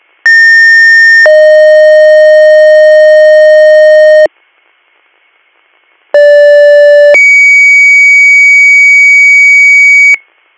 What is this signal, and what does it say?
Two-tone page, heard as a one-second tone, then a three-second tone, about 1800 then 620 Hz, about 580 then 2250 Hz. It is heard from 0.3 to 10.2 s.